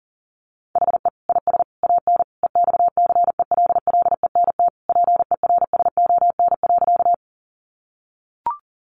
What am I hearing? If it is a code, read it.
Morse code, heard as a keyed tone, about 40 wpm, 700 Hz. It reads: HE IS AN EXCELLENT PERSON.
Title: Sentences from Top 1000 Words 40wpm